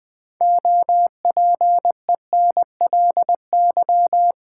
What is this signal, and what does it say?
Morse code, heard as a keyed tone, about 20 wpm, 700 Hz. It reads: OPENLY
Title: Build A Word - 2 Letters At A Time - 6-Letter Words - 20wpm